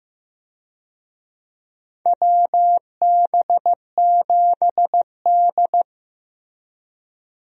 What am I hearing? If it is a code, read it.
Morse code, heard as a keyed tone, about 15 wpm, 700 Hz. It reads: WB7D
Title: USA Call Signs - 4 Characters - Repeated 3x - 15wpm